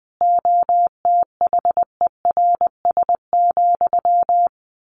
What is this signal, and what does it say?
Morse code, heard as a keyed tone, about 20 wpm, 700 Hz. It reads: OTHERS,
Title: A Sound Of Thunder 20wpm